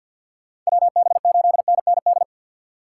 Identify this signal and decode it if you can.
Morse code, heard as a keyed tone, about 58 wpm, 700 Hz. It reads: WB8NND
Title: Speed Racing - USA Call Signs 50wpm